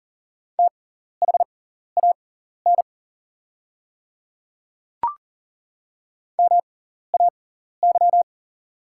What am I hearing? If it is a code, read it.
Morse code, heard as a keyed tone, about 40 wpm, spaced as for 14 wpm, 700 Hz. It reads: THAN MAY
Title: 6x Character Spacing - Top 200 Words - 40wpm